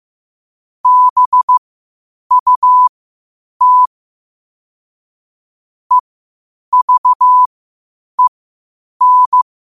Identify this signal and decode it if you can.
Morse code, heard as a keyed tone, about 15 wpm, spaced as for 8 wpm, 1000 Hz. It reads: BUT EVEN
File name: qotdfiles / 8wpm